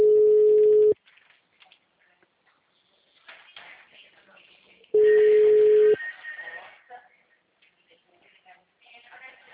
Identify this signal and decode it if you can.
Ringback tone (Europe, 425 Hz), with speech.